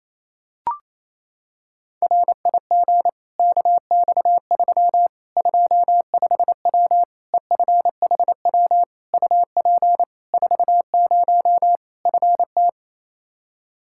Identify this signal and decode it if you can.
Morse code, heard as a keyed tone, about 28 wpm, 700 Hz. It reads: RIG KX3 25W EFHW UP 40 FT
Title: Practice QSO - Exchange 2 - Info - 28wpm